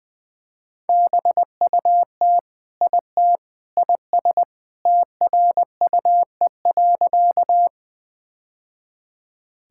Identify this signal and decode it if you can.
Morse code, heard as a keyed tone, about 20 wpm, 700 Hz. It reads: BUT IT IS TRUE.
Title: Sentences with 3-Word Phrases - 20wpm